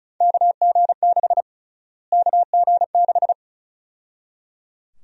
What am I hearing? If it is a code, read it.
Morse code, heard as a keyed tone, about 34 wpm, 700 Hz. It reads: KG6 KG6
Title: USA Call Sign Prefixes - Repeated 2x - 35wpm